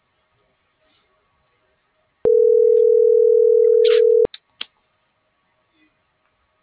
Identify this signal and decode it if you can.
Ringback tone (North America), 440 + 480 Hz.
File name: CENTRAL TERRESTRE / out lucero.sky